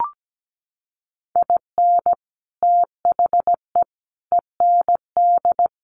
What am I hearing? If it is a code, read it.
Morse code, heard as a keyed tone, about 17 wpm, 700 Hz. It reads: IN THE END
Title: Sets of 3 Words 17wpm